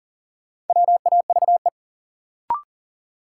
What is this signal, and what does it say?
Morse code, heard as a keyed tone, about 40 wpm, 700 Hz. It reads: WAVE
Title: Sentences from Top 1400 Words - New Word Drill - 40wpm